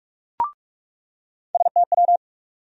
Morse code, heard as a keyed tone, about 46 wpm, 700 Hz. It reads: STW
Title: USA Call Sign Postfixes - 3 Letters 45wpm